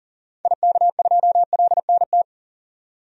Morse code, heard as a keyed tone, about 40 wpm, 700 Hz. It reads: IK2LNT
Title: DXCC-100 Call Signs - 6 Characters - 40wpm